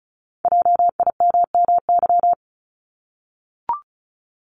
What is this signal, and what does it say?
Morse code, heard as a keyed tone, about 36 wpm, 700 Hz. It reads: JIMMY
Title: Multi-letter Phonemes Repeated 3x 35wpm